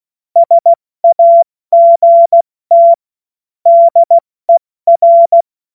Morse code, heard as a keyed tone, about 16 wpm, 680 Hz. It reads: SAGT DER
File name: Morsekurs / Witze / Witz 11. Mai 2021, 70 F80